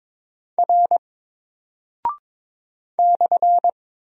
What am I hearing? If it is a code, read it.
Morse code, heard as a keyed tone, about 23 wpm, 700 Hz. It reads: R /
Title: Single Character (Letter:Number:Punctuation:Prosign) 22wpm